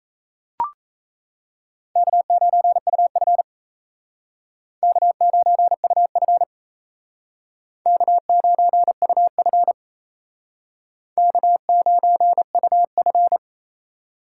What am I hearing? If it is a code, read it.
Morse code, heard as a keyed tone, about 36 wpm, 700 Hz. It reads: K9UF K9UF K9UF K9UF